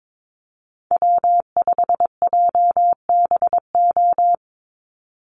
Morse code, heard as a keyed tone, about 22 wpm, 700 Hz. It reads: W5JBO
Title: Call Signs Repeated 3x - 22wpm